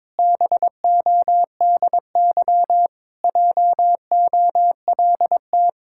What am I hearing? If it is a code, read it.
Morse code, heard as a keyed tone, about 22 wpm, 700 Hz. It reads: BODY JOLT
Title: Pangram Challenge v2 - 22wpm